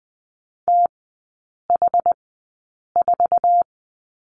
Morse code, heard as a keyed tone, about 20 wpm, spaced as for 8 wpm, 700 Hz. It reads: TH4
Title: 107 Lesson 18e - Letter m - Three Characters 20wpm